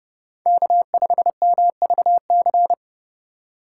Morse code, heard as a keyed tone, about 35 wpm, 700 Hz. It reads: K5MVC
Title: Speed Racing - USA Call Signs 30wpm